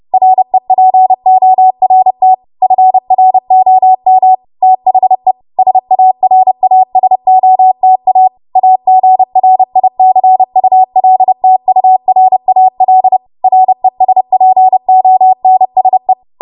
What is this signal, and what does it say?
Morse code, heard as a keyed tone, about 30 wpm, 750 Hz. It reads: REPORT FROM THE SARASOTA AGRICULTURAL RESPONSE
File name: Morse / 30WPM